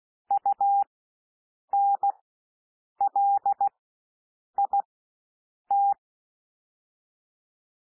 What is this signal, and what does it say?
Morse code, heard as a keyed tone, about 16 wpm, spaced as for 7 wpm, 800 Hz. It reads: UNLIT
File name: SOUND WORDS 16 wpm